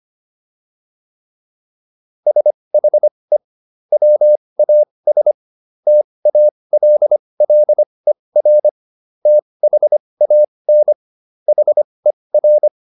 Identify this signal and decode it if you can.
Morse code, heard as a keyed tone, about 25 wpm, spaced as for 20 wpm, 600 Hz. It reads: SHE WAS TALLER THAN HER